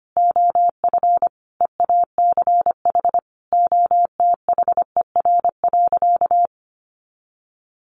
Morse code, heard as a keyed tone, about 25 wpm, 700 Hz. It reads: OF EACH OTHER.
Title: Sentences from Top 700 Words 25wpm